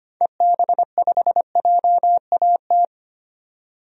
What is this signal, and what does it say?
Morse code, heard as a keyed tone, about 25 wpm, 700 Hz. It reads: EB5JAT
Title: DXCC-100 Call Signs - 6 Characters - 25wpm